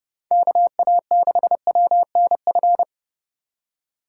Morse code, heard as a keyed tone, about 28 wpm, 700 Hz. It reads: KA6WNF